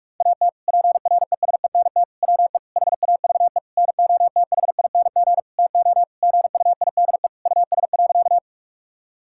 Morse code, heard as a keyed tone, about 46 wpm, 700 Hz. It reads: AT PRESENT WE HAVE NOTHING TO GUIDE US.
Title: Sentences from Top 1000 Words - Review Part 5 - 45wpm